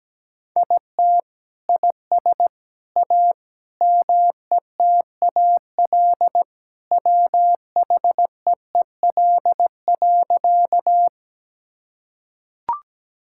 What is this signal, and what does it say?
Morse code, heard as a keyed tone, about 17 wpm, 700 Hz. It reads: IT IS A METAL WHEEL.
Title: Sentences from Top 1000 Words - Review Part 6 - 17wpm